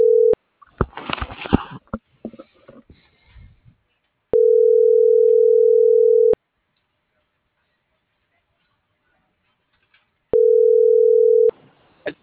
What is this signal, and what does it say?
Ringback tone (North America), 440 + 480 Hz.
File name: CENTRAL SOTOVA / out alejandro.rodriguez - .50523